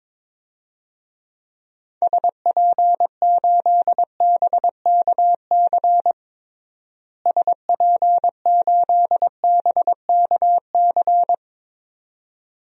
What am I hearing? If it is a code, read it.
Morse code, heard as a keyed tone, about 21 wpm, 700 Hz. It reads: SP8BKC SP8BKC
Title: DXCC-100 Call Signs - 6 Characters - Repeated 2x - 22wpm